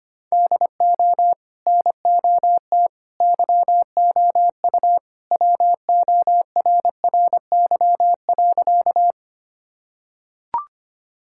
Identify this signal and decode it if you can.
Morse code, heard as a keyed tone, about 25 wpm, 700 Hz. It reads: DO NOT YOU WORRY.
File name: practice / 25wpm / Sentences with 3-Word Phrases - 25wpm